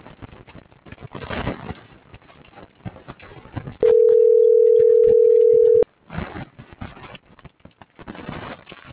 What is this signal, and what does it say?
Ringback tone (North America), 440 + 480 Hz.